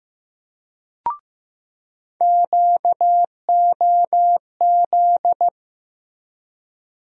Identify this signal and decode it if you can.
Morse code, heard as a keyed tone, about 15 wpm, 700 Hz. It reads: QOZ
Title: USA Call Sign Postfixes - 3 Letters 15wpm